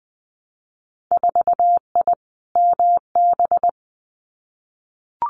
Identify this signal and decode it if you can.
Morse code, heard as a keyed tone, about 20 wpm, 700 Hz. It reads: 4I MB
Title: Winter Field Day Exchanges Encore 20wpm